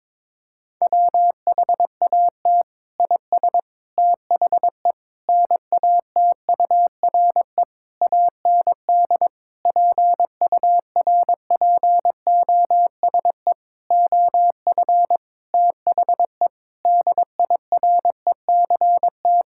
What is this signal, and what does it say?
Morse code, heard as a keyed tone, about 22 wpm, 700 Hz. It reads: WHAT IS THE NATURE AND PURPOSE OF THE DIRECT
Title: Sentences from Top 1000 Words - Review Part 5 - 22wpm